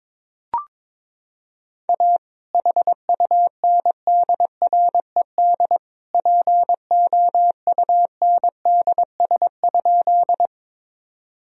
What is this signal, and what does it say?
Morse code, heard as a keyed tone, about 22 wpm, 700 Hz. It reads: A HUNDRED POUNDS?
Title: Sentences from Top 1000 Words - Review Part 2 - 22wpm